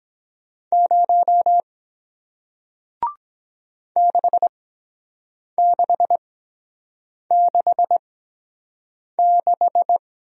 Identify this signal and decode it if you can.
Morse code, heard as a keyed tone, about 24 wpm, spaced as for 7 wpm, 700 Hz. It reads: 0 6666